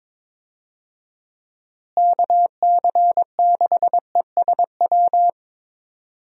Morse code, heard as a keyed tone, about 22 wpm, 700 Hz. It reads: KC6ESW
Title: USA Call Signs - 6 Characters - Repeated 2x - 22wpm